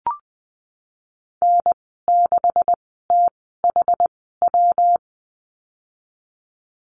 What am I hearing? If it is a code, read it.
Morse code, heard as a keyed tone, about 20 wpm, spaced as for 15 wpm, 700 Hz. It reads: N6THW
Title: 140 Lesson 23g - Letter f - Call Signs 20wpm